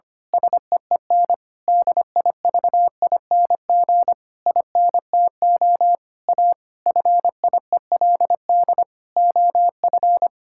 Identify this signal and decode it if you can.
Morse code, heard as a keyed tone, about 25 wpm, 700 Hz. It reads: SEEN DIVING INTO A FIELD OF
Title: Pangram Challenge v2 - 25wpm